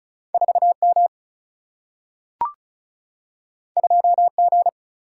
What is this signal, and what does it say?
Morse code, heard as a keyed tone, about 36 wpm, 700 Hz. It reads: 4M 2G
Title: Single-Letter Single-Number 35wpm